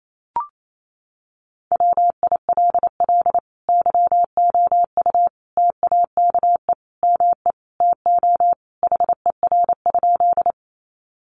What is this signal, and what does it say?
Morse code, heard as a keyed tone, about 28 wpm, 700 Hz. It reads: WILL YOU TAKE ME TO HER?
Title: Sentences from Top 200 Words 28wpm